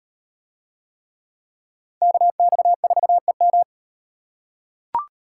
Morse code, heard as a keyed tone, about 36 wpm, 700 Hz. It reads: KX4EM